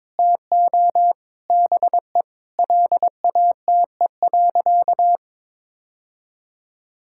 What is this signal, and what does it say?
Morse code, heard as a keyed tone, about 22 wpm, 700 Hz. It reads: TO BE LATE.
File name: practice / 22wpm / Sentences with 4-Word Phrases - 22wpm